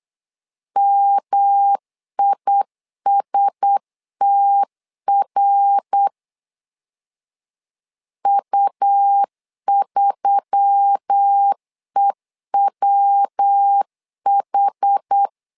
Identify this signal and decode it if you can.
Morse code, heard as a keyed tone, about 8 wpm, 800 Hz. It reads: MISTR U3EWH
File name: LEZIONI / 4 Q1S3 8wpm 3037